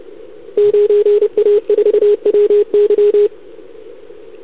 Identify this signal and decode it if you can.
Morse code, heard as a keyed tone, about 30 wpm, 410 Hz. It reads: 9A4WY
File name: CQWWCW2021 / 4X6FR